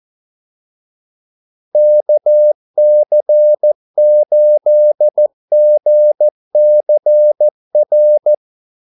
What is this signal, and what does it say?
Morse code, heard as a keyed tone, about 14 wpm, 600 Hz. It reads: KC8GCR
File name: Callsigns-14wpm